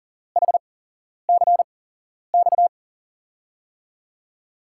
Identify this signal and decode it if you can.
Morse code, heard as a keyed tone, about 40 wpm, spaced as for 11 wpm, 700 Hz. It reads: HCX
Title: USA Call Signs - 8x Character Spacing - 40wpm